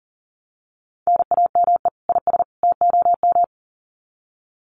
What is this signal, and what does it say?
Morse code, heard as a keyed tone, about 40 wpm, 700 Hz. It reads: NAME IS TOM
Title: Common QSO Phrases 40wpm